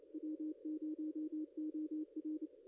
Morse code, heard as a keyed tone, about 23 wpm, 310 Hz. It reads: W0OR
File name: contests / ss-cw-k5tr-op-n5rz -0900